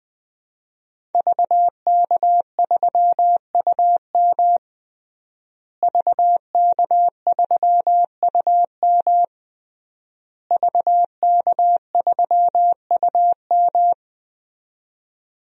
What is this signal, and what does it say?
Morse code, heard as a keyed tone, about 21 wpm, 700 Hz. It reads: VK3UM VK3UM VK3UM
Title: DXCC-100 Call Signs - 5 Characters - Repeated 3x - 20wpm